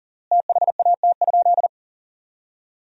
Morse code, heard as a keyed tone, about 37 wpm, 700 Hz. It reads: THAT?
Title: 3x Word-Spacing - Sentences from Top 300 Words - 40wpm